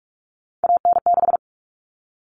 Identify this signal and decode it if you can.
Morse code, heard as a keyed tone, about 44 wpm, 700 Hz. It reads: AN6
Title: All Cut Numbers - 3-Digit 45wpm